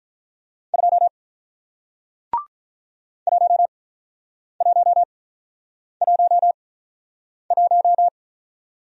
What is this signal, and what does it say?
Morse code, heard as a keyed tone, about 48 wpm, spaced as for 9 wpm, 700 Hz. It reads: 2 1111